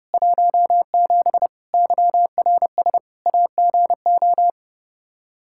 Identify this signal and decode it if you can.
Morse code, heard as a keyed tone, about 30 wpm, 700 Hz. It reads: 17 YRS AGO
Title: K1USN SST QSO Practice 30wpm